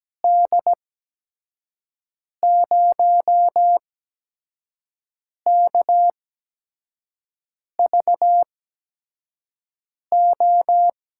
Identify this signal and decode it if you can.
Morse code, heard as a keyed tone, about 17 wpm, spaced as for 5 wpm, 700 Hz. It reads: D0KVO
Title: USA Call Signs - 8x Character Spacing - 17wpm